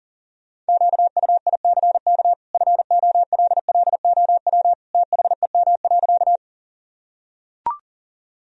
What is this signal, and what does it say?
Morse code, heard as a keyed tone, about 40 wpm, 700 Hz. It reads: QUICK FOLLOW THEM.
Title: Sentences from Top 1000 Words - Review Part 1 - 40wpm